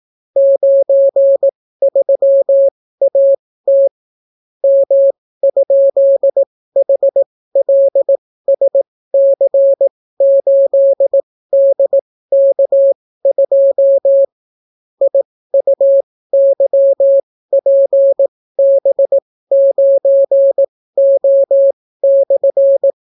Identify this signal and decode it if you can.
Morse code, heard as a keyed tone, about 18 wpm, spaced as for 15 wpm, 550 Hz. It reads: 93AT M?HLSC8DK2 IUYPB9O/